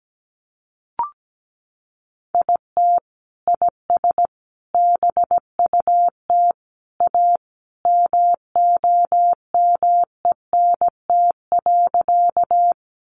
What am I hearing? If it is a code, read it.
Morse code, heard as a keyed tone, about 17 wpm, 700 Hz. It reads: IT IS BUT A MOMENT.
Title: Sentences from Top 1000 Words 17wpm